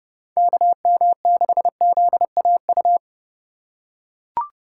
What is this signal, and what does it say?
Morse code, heard as a keyed tone, about 30 wpm, 700 Hz. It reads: KM6ZAU